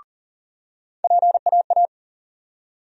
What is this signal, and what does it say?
Morse code, heard as a keyed tone, about 40 wpm, 700 Hz. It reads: PAA
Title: USA Call Sign Postfixes - 3 Letters 40wpm